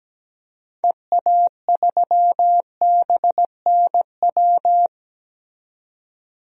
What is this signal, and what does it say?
Morse code, heard as a keyed tone, about 17 wpm, 700 Hz. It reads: EA3BNW
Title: DXCC-100 Call Signs - 6 Characters - 17wpm